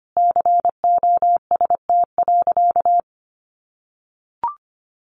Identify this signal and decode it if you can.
Morse code, heard as a keyed tone, about 25 wpm, 700 Hz. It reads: COST.
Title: Sentences from Top 900 Words 25wpm